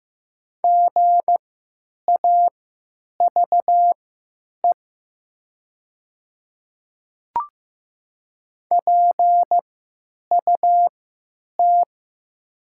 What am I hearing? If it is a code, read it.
Morse code, heard as a keyed tone, about 15 wpm, spaced as for 9 wpm, 700 Hz. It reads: GAVE PUT
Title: 3x Character Spacing - Top 500 Words - 15wpm